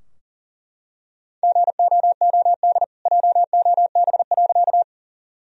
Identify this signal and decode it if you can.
Morse code, heard as a keyed tone, about 40 wpm, 700 Hz. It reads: GOOD JOB.
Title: Sentences from Top 1000 Words - Review Part 3 - 40wpm